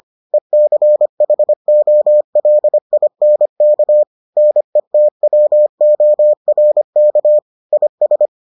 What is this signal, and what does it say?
Morse code, heard as a keyed tone, about 25 wpm, 600 Hz. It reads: ECHOLINK NETWORK IS